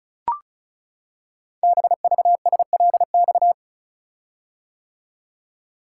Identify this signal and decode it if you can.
Morse code, heard as a keyed tone, about 36 wpm, 700 Hz. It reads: BVSLX
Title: Code Groups - 5 Letter:Numbers - 35wpm